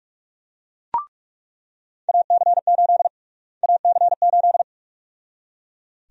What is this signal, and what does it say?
Morse code, heard as a keyed tone, about 41 wpm, 700 Hz. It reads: AC8 AC8
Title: USA Call Sign Prefixes - Repeated 2x - 45wpm